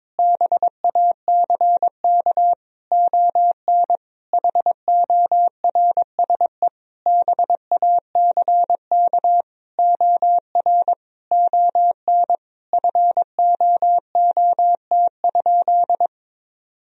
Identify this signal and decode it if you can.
Morse code, heard as a keyed tone, about 21 wpm, 700 Hz. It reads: BACK ON HORSE BACK OR ON FOOT?
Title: Sentences from Top 1000 Words - Review Part 3 - 22wpm